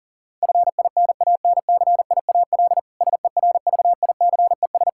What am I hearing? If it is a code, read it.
Morse code, heard as a keyed tone, about 40 wpm, 700 Hz. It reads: FINANCIAL SERVICES
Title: K1USN SST QSO Practice 40wpm